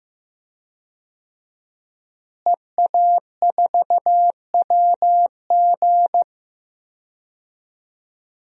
Morse code, heard as a keyed tone, about 15 wpm, 700 Hz. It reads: EA4WG
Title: DXCC-100 Call Signs - 5 Characters - Repeated 3x - 15wpm